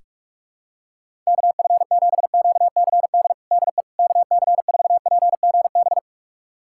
Morse code, heard as a keyed tone, about 47 wpm, 700 Hz. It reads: KF7QCD DE KC4PGB
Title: USA Callsign DE Callsign 45wpm